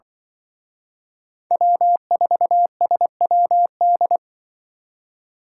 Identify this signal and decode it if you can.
Morse code, heard as a keyed tone, about 24 wpm, 700 Hz. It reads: W4SWD
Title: Speed Racing - USA Call Signs 20wpm